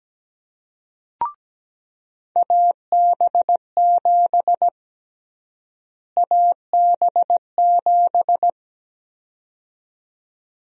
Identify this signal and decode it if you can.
Morse code, heard as a keyed tone, about 17 wpm, 700 Hz. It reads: AB7 AB7
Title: USA Call Sign Prefixes - Repeated 2x - 17wpm